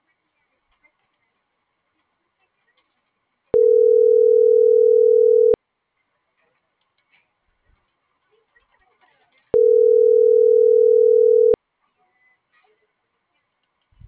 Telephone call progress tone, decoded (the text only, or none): ringback tone